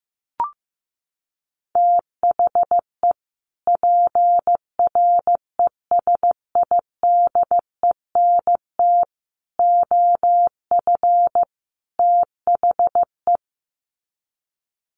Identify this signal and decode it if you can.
Morse code, heard as a keyed tone, about 15 wpm, 700 Hz. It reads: THE PRESIDENT OF THE
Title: Sets of 4 Words 15wpm